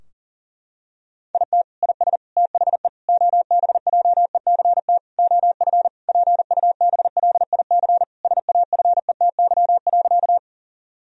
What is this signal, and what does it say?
Morse code, heard as a keyed tone, about 37 wpm, 700 Hz. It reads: IT IS THE OBJECT OF PUBLIC SAFETY.
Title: Sentences from Top 1000 Words - Review Part 4 - 40wpm